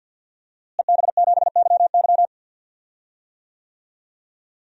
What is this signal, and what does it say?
Morse code, heard as a keyed tone, about 48 wpm, 700 Hz. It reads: EB7YY
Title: DXCC-100 Call Signs - 5 Characters - Repeated 3x - 50wpm